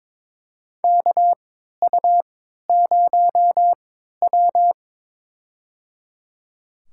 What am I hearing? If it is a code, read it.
Morse code, heard as a keyed tone, about 22 wpm, spaced as for 12 wpm, 700 Hz. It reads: KU0W
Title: USA Call Signs - 4 Characters - 3x Character Spacing - 22wpm